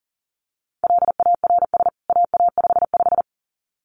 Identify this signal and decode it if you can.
Morse code, heard as a keyed tone, about 40 wpm, 700 Hz. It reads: LARS AA55
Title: CWT Exchange 40wpm